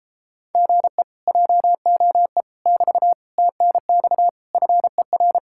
Morse code, heard as a keyed tone, about 33 wpm, 700 Hz. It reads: GE JOE <BT> TNX FER